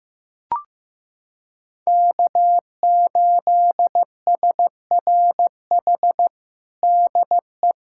Morse code, heard as a keyed tone, about 15 wpm, 700 Hz. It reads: K8SRH DE
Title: USA Callsign DE Callsign 15wpm